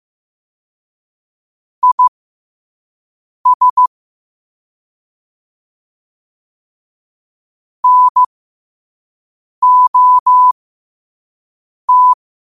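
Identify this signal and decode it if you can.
Morse code, heard as a keyed tone, about 15 wpm, spaced as for 5 wpm, 1000 Hz. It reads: IS NOT
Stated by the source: Morse code at 5 words per minute.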